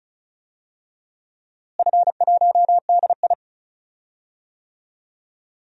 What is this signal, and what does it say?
Morse code, heard as a keyed tone, about 36 wpm, 700 Hz. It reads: F1DI
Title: DXCC-100 Call Signs - 4 Characters - Repeated 3x - 35wpm